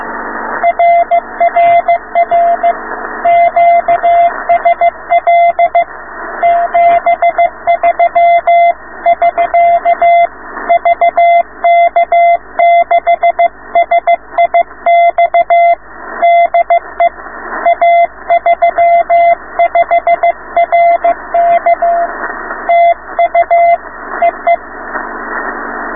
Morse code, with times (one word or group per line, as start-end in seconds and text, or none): 0.6-2.7 RRR
3.3-5.8 QSL
6.4-10.3 73<SK>
10.7-15.7 VK6SIX
16.2-17.1 DE
17.7-22.1 A35RK
22.7-23.8 TU
24.2-24.6 EE